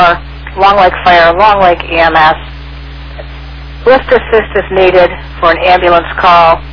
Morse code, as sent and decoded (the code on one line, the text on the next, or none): none